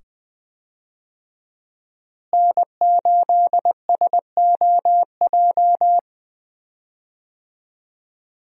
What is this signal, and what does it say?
Morse code, heard as a keyed tone, about 20 wpm, 700 Hz. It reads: N8SOJ
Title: USA Call Signs - 5 Characters - Repeated 2x - 20wpm